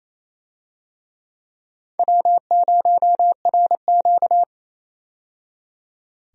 Morse code, heard as a keyed tone, about 28 wpm, 700 Hz. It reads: W0RQ
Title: USA Call Signs - 4 Characters - 28wpm